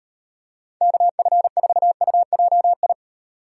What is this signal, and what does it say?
Morse code, heard as a keyed tone, about 38 wpm, 700 Hz. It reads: KF4UJI